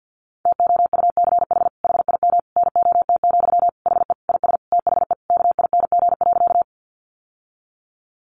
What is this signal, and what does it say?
Morse code, heard as a keyed tone, about 48 wpm, 700 Hz. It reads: TOUCH HIM NOT, HE IS THE KING.
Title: Sentences from Top 1200 Words 50wpm